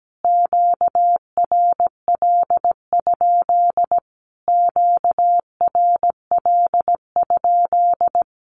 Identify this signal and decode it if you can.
Morse code, heard as a keyed tone, about 17 wpm, 700 Hz. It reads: QRL? QRL?
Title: Q-Codes Repeated 3x 17wpm